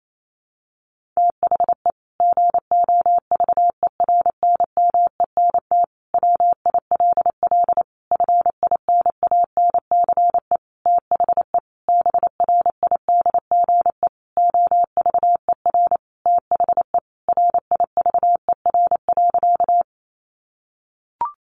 Morse code, heard as a keyed tone, about 28 wpm, 700 Hz. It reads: THE GOVERNMENT WILL FINANCE THE BRIDGE OVER THE RIVER.
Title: Sentences from Top 1500 Words 28wpm